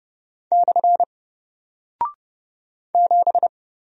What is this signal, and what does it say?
Morse code, heard as a keyed tone, about 29 wpm, 700 Hz. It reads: / 7
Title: Single Character (Letter:Number:Punctuation:Prosign) 30wpm